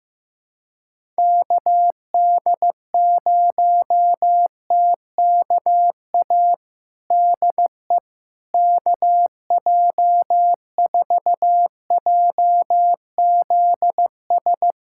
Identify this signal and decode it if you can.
Morse code, heard as a keyed tone, about 15 wpm, 700 Hz. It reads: KD0TKA DE KJ4JZS
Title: USA Callsign DE Callsign - 6 Characters - 15wpm